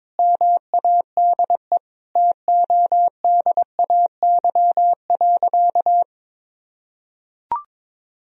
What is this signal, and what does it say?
Morse code, heard as a keyed tone, about 22 wpm, 700 Hz. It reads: MADE TODAY.
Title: Sentences from Top 1000 Words - Review Part 2 - 22wpm